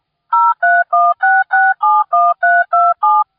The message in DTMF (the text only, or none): 0316671327